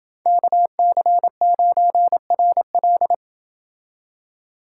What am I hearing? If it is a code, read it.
Morse code, heard as a keyed tone, about 28 wpm, 700 Hz. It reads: KC9RL